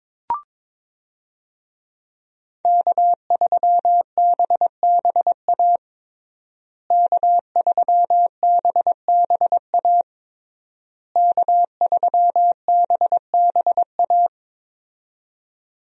Morse code, heard as a keyed tone, about 22 wpm, 700 Hz. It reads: K3BBA K3BBA K3BBA
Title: USA Call Signs - 5 Characters - Repeated 3x - 22wpm